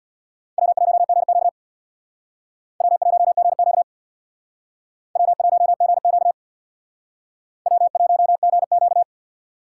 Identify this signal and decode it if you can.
Morse code, heard as a keyed tone, about 62 wpm, 700 Hz. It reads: W1GQ W1GQ W1GQ W1GQ
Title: Speed Racing - USA 1x2 Call Signs 50wpm